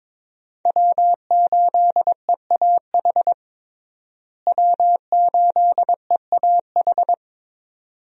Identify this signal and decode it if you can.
Morse code, heard as a keyed tone, about 22 wpm, 700 Hz. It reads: W8EAH W8EAH
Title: USA Call Signs - 5 Characters - Repeated 2x - 22wpm